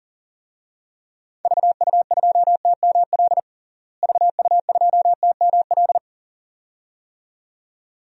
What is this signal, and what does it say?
Morse code, heard as a keyed tone, about 37 wpm, 700 Hz. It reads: VU2TML VU2TML
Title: DXCC-100 Call Signs - 6 Characters - Repeated 2x - 40wpm